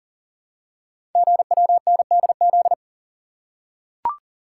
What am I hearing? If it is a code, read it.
Morse code, heard as a keyed tone, about 40 wpm, 700 Hz. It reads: GWNDZ
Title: Code Groups - 5 Letters - 40wpm